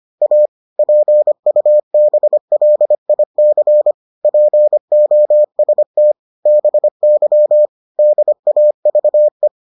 Morse code, heard as a keyed tone, about 25 wpm, 600 Hz. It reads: A PUBLIC POST BY DAVE